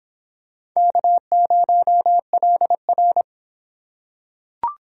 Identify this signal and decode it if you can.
Morse code, heard as a keyed tone, about 22 wpm, 700 Hz. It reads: K0LR